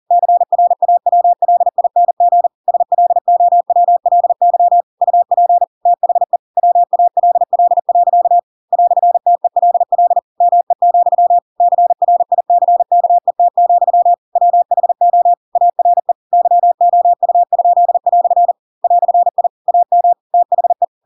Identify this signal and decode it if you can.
Morse code, heard as a keyed tone, about 40 wpm, 700 Hz. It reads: CRAWLING SLOWLY UP THE WALL. "TELL ME, CRICKET, WHO ARE YOU?" "I AM THE
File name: Pinocchio / Pinocchio ch4 40 40 ik1hge